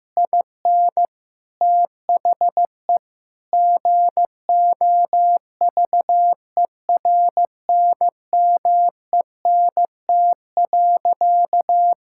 Morse code, heard as a keyed tone, about 15 wpm, 700 Hz. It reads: IN THE GOVERNMENT.
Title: Sentences from Top 1600 Words 15wpm